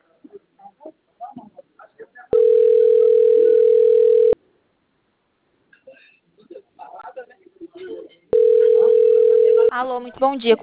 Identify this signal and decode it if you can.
Ringback tone (North America), 440 + 480 Hz, with speech.
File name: out- - - - - .84301